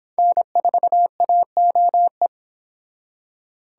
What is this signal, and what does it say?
Morse code, heard as a keyed tone, about 25 wpm, 700 Hz. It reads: N4AOE